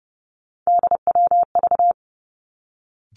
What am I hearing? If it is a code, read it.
Morse code, heard as a keyed tone, about 30 wpm, 700 Hz. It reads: DWV
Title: Three Challenging Letter-Number Combinations 30wpm